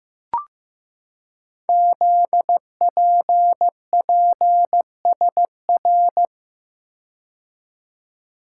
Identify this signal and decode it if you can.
Morse code, heard as a keyed tone, about 15 wpm, 700 Hz. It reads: ZPPSR
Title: Code Groups - 5 Letters - 15wpm